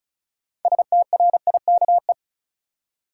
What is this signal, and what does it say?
Morse code, heard as a keyed tone, about 36 wpm, 700 Hz. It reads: STRIKE
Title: Six Letter Words - 35wpm